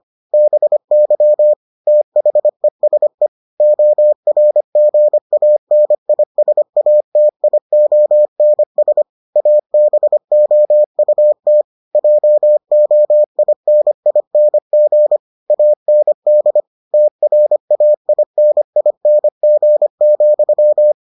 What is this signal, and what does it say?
Morse code, heard as a keyed tone, about 25 wpm, 600 Hz. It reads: BY THESE ORGANISATIONS ABOUT JOINING AND TRAINING,